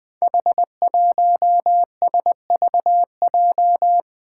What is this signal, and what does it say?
Morse code, heard as a keyed tone, about 20 wpm, 700 Hz. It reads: H1SVJ
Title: Five Challenging Letter-Number Combinations 20wpm